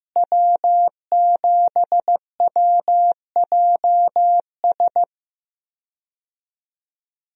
Morse code, heard as a keyed tone, about 15 wpm, 700 Hz. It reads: W7WJS
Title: USA Call Signs - 5 Characters - Repeated 3x - 15wpm